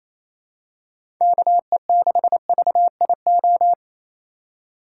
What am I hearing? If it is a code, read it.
Morse code, heard as a keyed tone, about 28 wpm, 700 Hz. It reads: KE6VIO
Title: USA Call Signs - 6 Characters - 2x Character Spacing - 28wpm